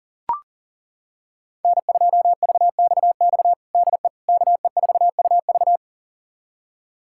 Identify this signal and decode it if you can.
Morse code, heard as a keyed tone, about 37 wpm, 700 Hz. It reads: N2VXX DE KE4UV
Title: USA Callsign DE Callsign - 5 Characters - 40wpm